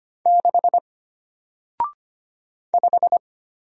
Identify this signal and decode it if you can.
Morse code, heard as a keyed tone, about 25 wpm, 700 Hz. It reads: 6 5
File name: Single-Letter-Number-25wpm